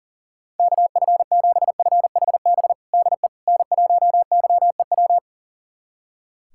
Morse code, heard as a keyed tone, about 40 wpm, 700 Hz. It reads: KF7FHB DE N1YEW
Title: USA Callsign DE Callsign 40wpm